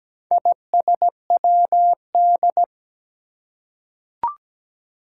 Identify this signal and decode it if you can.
Morse code, heard as a keyed tone, about 17 wpm, 700 Hz. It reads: ISWD
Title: Four Challenging Letter-Number Combinations 17wpm